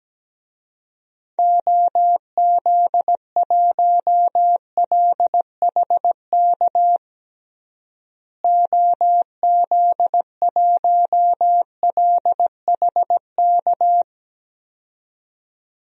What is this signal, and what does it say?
Morse code, heard as a keyed tone, about 17 wpm, 700 Hz. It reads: OZ1LHK OZ1LHK
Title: DXCC-100 Call Signs - 6 Characters - Repeated 2x - 17wpm